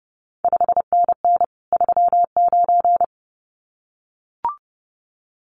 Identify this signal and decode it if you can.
Morse code, heard as a keyed tone, about 30 wpm, 700 Hz. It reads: 5NN 39
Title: CQ WW DX Contest Exchange - 30wpm